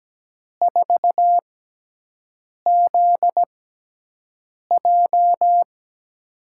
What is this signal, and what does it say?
Morse code, heard as a keyed tone, about 17 wpm, spaced as for 6 wpm, 700 Hz. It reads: 4ZJ
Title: USA Call Signs - 6x Character Spacing - 17wpm